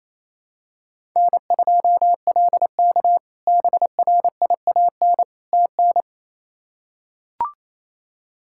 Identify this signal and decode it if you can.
Morse code, heard as a keyed tone, about 28 wpm, 700 Hz. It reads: N2LK BRIAN TN
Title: K1USN SST QSO Practice - 28wpm